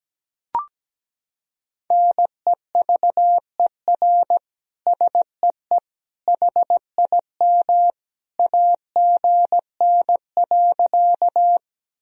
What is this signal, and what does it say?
Morse code, heard as a keyed tone, about 17 wpm, 700 Hz. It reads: NEVER SEE HIM AGN.
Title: Sentences from Top 300 Words and Abbreviations 17wpm